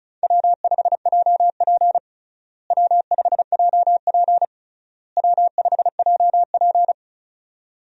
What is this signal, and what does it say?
Morse code, heard as a keyed tone, about 34 wpm, 700 Hz. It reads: W5JP W5JP W5JP
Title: USA Call Signs - 4 Characters - Repeated 3x - 35wpm